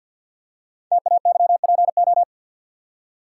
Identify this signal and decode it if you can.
Morse code, heard as a keyed tone, about 51 wpm, 700 Hz. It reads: TAYPO
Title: Code Groups - 5 Letters - 50wpm